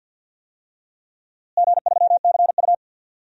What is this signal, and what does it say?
Morse code, heard as a keyed tone, about 48 wpm, 700 Hz. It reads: G3CU
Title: DXCC-100 Call Signs - 4 Characters - Repeated 3x - 50wpm